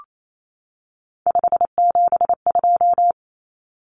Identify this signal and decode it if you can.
Morse code, heard as a keyed tone, about 28 wpm, 700 Hz. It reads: 572
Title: Common Cut Numbers - 3-Digit 28wpm